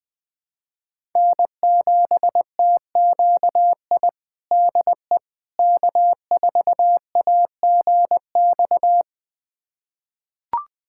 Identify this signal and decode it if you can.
Morse code, heard as a keyed tone, about 19 wpm, 700 Hz. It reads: N7TQI DE K4AGX
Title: USA Callsign DE Callsign 20wpm